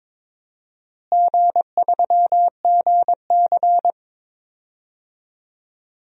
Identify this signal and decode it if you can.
Morse code, heard as a keyed tone, about 22 wpm, 700 Hz. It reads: G3GC
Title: DXCC-100 Call Signs - 4 Characters - Repeated 3x - 22wpm